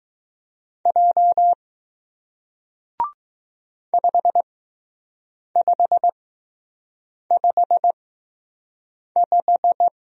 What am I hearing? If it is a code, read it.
Morse code, heard as a keyed tone, about 20 wpm, spaced as for 6 wpm, 700 Hz. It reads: J 5555